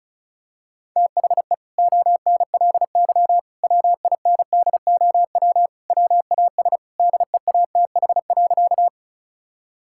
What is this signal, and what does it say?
Morse code, heard as a keyed tone, about 34 wpm, 700 Hz. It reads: THE ONLY WINDOW WAS DEATH.
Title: Sentences from Top 1600 Words 35wpm